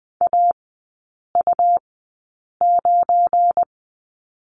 Morse code, heard as a keyed tone, about 20 wpm, spaced as for 8 wpm, 700 Hz. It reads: AU9
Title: 212 Lesson 34e - Number 8 - Three Characters 20wpm